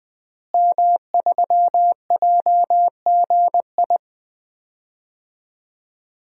Morse code, heard as a keyed tone, about 20 wpm, 700 Hz. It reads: M3JGI
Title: DXCC-100 Call Signs - 5 Characters - Repeated 2x - 20wpm